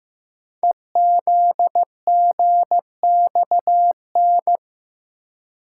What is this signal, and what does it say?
Morse code, heard as a keyed tone, about 15 wpm, 700 Hz. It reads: EZGXN
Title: Code Groups - 5 Letter:Numbers - 15wpm